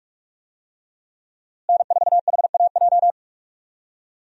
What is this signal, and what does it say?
Morse code, heard as a keyed tone, about 42 wpm, 700 Hz. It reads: N4HAJ
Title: USA Call Signs - 5 Characters - Repeated 3x - 45wpm